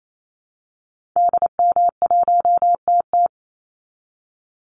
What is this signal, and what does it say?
Morse code, heard as a keyed tone, about 28 wpm, 700 Hz. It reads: DM1TT
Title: Call Signs Repeated 3x - 28wpm